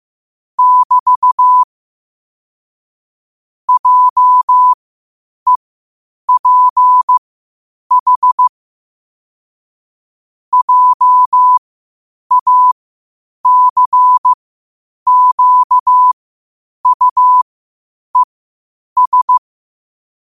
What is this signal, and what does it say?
Morse code, heard as a keyed tone, about 15 wpm, spaced as for 8 wpm, 1000 Hz. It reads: <BT> JEPH JACQUES
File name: qotdfiles / 8wpm